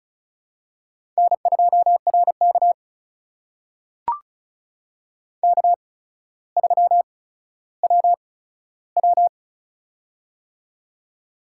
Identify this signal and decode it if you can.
Morse code, heard as a keyed tone, about 34 wpm, spaced as for 15 wpm, 700 Hz. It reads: N2RK K3WW
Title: USA Call Signs - 4 Characters - 8x Character Spacing - 35wpm